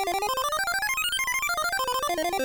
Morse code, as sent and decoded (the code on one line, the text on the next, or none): none